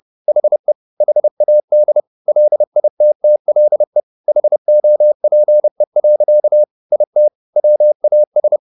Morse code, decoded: HE HAD LITTLE HOPE. IT WAS

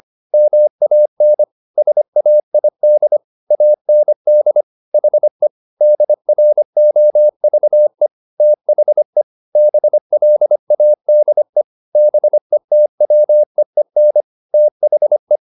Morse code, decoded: MAN SAID AND HE DROVE THE BLADE BETWEEN THE

